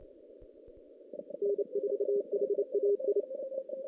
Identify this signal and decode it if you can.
Morse code, heard as a keyed tone, about 29 wpm, 400 Hz. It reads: N4HAI
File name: 2016 JAN CW NAQP / 2016 Jan naqp k5ot - 0-5